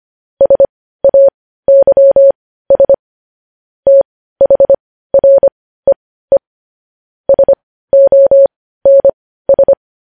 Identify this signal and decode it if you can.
Morse code, heard as a keyed tone, about 25 wpm, spaced as for 15 wpm, 550 Hz. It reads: SAYS THREE SONS
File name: podcast / News headlines-1x-25x15